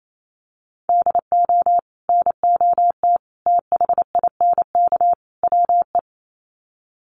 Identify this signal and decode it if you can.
Morse code, heard as a keyed tone, about 28 wpm, 700 Hz. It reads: DO NOT THINK WE
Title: Sets of 4 Words 28wpm